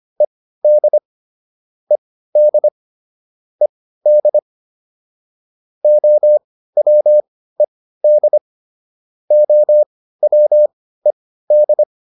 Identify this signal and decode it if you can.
Morse code, heard as a keyed tone, about 25 wpm, spaced as for 15 wpm, 600 Hz. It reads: ED ED ED OWED OWED